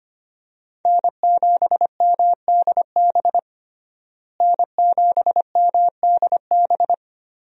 Morse code, heard as a keyed tone, about 25 wpm, 700 Hz. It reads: N7MDB N7MDB
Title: USA Call Signs - 5 Characters - Repeated 3x - 25wpm